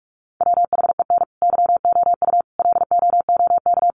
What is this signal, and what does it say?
Morse code, heard as a keyed tone, about 44 wpm, 700 Hz. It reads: WHEN YOU LOOK
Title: Sentences from Top 800 Words 45wpm